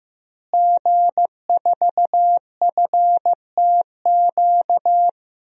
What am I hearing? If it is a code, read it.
Morse code, heard as a keyed tone, about 16 wpm, 700 Hz. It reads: G4FTQ
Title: DXCC-100 Call Signs - 5 Characters - Repeated 2x - 15wpm